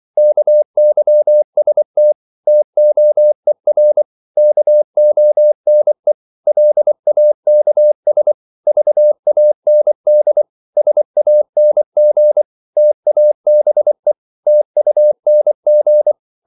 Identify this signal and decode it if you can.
Morse code, heard as a keyed tone, about 24 wpm, 600 Hz. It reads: KYST TOER KONE LAKS VAND SANG TABE TUNG
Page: Korte danske ord 24wpm